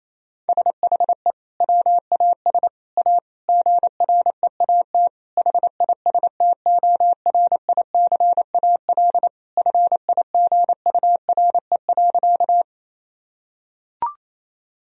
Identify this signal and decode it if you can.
Morse code, heard as a keyed tone, about 28 wpm, 700 Hz. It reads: SHE WAS A GREAT HISTORICAL FIGURE.
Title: Sentences from Top 1000 Words - Review Part 4 - 28wpm